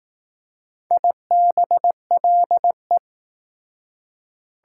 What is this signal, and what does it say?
Morse code, heard as a keyed tone, about 19 wpm, 700 Hz. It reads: IBLE